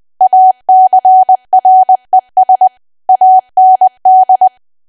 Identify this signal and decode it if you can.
Morse code, heard as a keyed tone, about 20 wpm, 750 Hz. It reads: ACRES AND
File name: Archive / 20 WPM / 20WPM